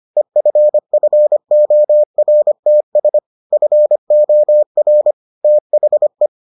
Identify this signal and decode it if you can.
Morse code, heard as a keyed tone, about 25 wpm, 600 Hz. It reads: EFFORTS FOR THE